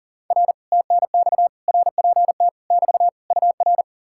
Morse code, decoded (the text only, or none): R TNX RPT <BT> UR